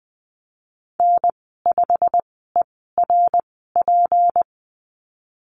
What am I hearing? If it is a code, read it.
Morse code, heard as a keyed tone, about 20 wpm, spaced as for 14 wpm, 700 Hz. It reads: N5ERP
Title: 154 Lesson 25g - Letter p - Call Signs 20wpm